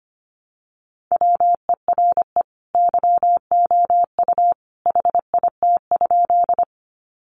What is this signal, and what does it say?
Morse code, heard as a keyed tone, about 25 wpm, 700 Hz. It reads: WERE YOU HIT?
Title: Sentences from Top 1200 Words 25wpm